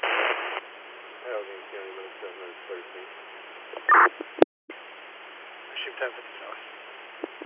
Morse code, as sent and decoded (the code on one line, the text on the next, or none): none